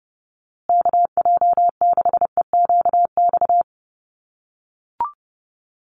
Morse code, heard as a keyed tone, about 30 wpm, 700 Hz. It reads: KJ6EQX
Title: USA Call Signs Repeated 2x 30wpm